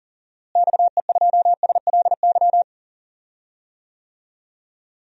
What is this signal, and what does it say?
Morse code, heard as a keyed tone, about 40 wpm, 700 Hz. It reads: XE2SLY
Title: DXCC-100 Call Signs - 6 Characters - Repeated 3x - 40wpm